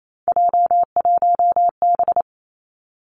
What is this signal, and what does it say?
Morse code, heard as a keyed tone, about 28 wpm, 700 Hz. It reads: J1B
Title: Three Challenging Letter-Number Combinations 28wpm